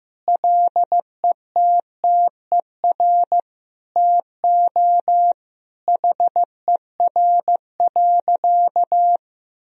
Morse code, heard as a keyed tone, about 15 wpm, 700 Hz. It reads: LETTER TO HER.
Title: Sentences from Top 1000 Words - Review Part 3 - 15wpm